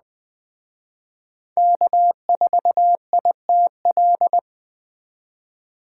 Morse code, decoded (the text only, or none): K4ITL